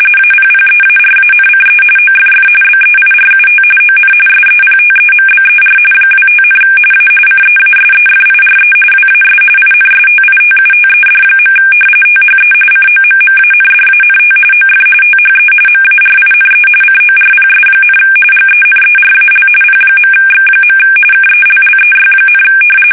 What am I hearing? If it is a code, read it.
RTTY, about 45.45 baud. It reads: APPEARANCE OFF BEIRUT ON SEPTEMBER 25, A CEASEFIRE WAS INSTITUTED. THE COMMANDANT OF THE MARINE CORPS, GENERAL P.X. KELLEY,
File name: fsk850 / battleshipnj